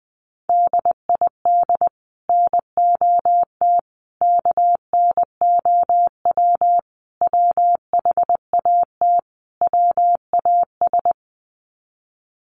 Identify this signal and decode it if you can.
Morse code, heard as a keyed tone, about 20 wpm, 700 Hz. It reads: DID NOT KNOW WHAT WAS
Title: Sets of 5 Words Encore2 20wpm